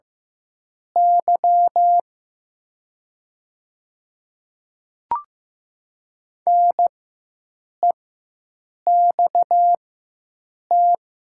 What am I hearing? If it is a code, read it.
Morse code, heard as a keyed tone, about 15 wpm, spaced as for 7 wpm, 700 Hz. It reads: Y NEXT
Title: 4x Character Spacing - Top 500 Words - 15wpm